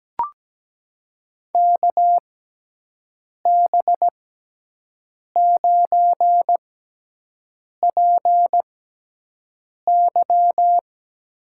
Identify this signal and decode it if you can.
Morse code, heard as a keyed tone, about 17 wpm, spaced as for 6 wpm, 700 Hz. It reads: KB9PY
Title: USA Call Signs - 6 Characters - 6x Character Spacing - 17wpm